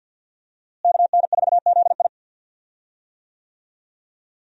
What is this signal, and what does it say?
Morse code, heard as a keyed tone, about 53 wpm, 700 Hz. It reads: KN4ZI
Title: USA Call Signs - 5 Characters - Repeated 2x - 50wpm